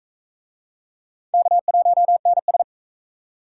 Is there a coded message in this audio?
Yes: Morse code from 1.3 to 2.6 s, heard as a keyed tone, about 42 wpm, 700 Hz. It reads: K1NS